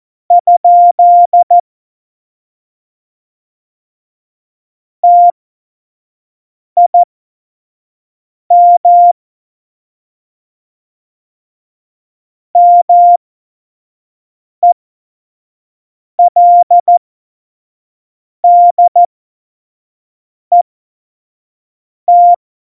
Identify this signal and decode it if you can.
Morse code, heard as a keyed tone, about 14 wpm, spaced as for 5 wpm, 700 Hz. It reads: ? TIM MELDET
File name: Morsekurs / Witze / Witz 23. April 2019, 25 F70